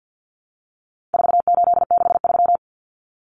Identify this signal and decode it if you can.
Morse code, heard as a keyed tone, about 49 wpm, 700 Hz. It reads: 4863
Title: Common Cut Numbers - 4-Digit 50wpm